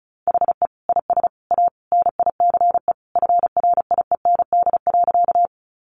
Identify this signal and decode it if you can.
Morse code, heard as a keyed tone, about 34 wpm, 700 Hz. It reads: HE IS A NICE FRIEND.
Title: Sentences from Top 1100 Words 35wpm